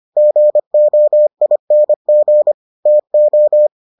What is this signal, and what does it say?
Morse code, heard as a keyed tone, about 25 wpm, 600 Hz. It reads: GOING TO